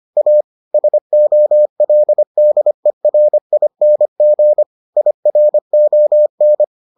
Morse code, heard as a keyed tone, about 25 wpm, 600 Hz. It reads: A SOLDERING IRON